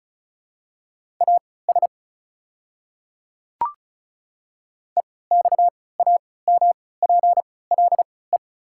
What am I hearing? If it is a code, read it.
Morse code, heard as a keyed tone, about 36 wpm, spaced as for 20 wpm, 690 Hz. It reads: AS EXAMPLE
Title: 3x Character Spacing - Top 300 Words - 35wpm